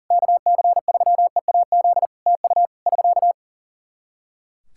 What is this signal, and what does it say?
Morse code, heard as a keyed tone, about 40 wpm, 700 Hz. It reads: KC3EAZ TU <SK>
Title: Practice QSO - Exchange 4 - Ending - 40wpm